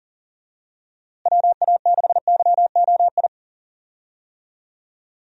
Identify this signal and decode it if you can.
Morse code, heard as a keyed tone, about 40 wpm, 700 Hz. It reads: WA6YOI
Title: USA Call Signs - 6 Characters - Repeated 3x - 40wpm